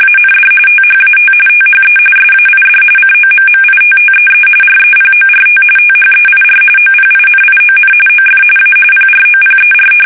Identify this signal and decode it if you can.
RTTY, about 45.45 baud. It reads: AND THAT THE SAID COMMITTEE REPORT THEIR OPINION OF A PROPER
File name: fsk850 / usnavy1